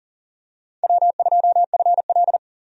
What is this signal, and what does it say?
Morse code, heard as a keyed tone, about 40 wpm, 700 Hz. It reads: W2FL
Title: USA Call Signs - 4 Characters - 40wpm